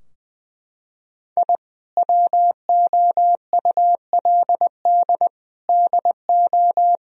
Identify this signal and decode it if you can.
Morse code, heard as a keyed tone, about 20 wpm, 700 Hz. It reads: I WOULD DO
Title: Sentences from Top 1000 Words - Review Part 2 - 20wpm